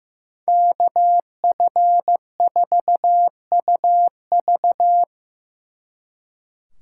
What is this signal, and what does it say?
Morse code, heard as a keyed tone, about 18 wpm, 700 Hz. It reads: KF4UV
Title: Speed Racing - USA 2x2 Call Signs 15wpm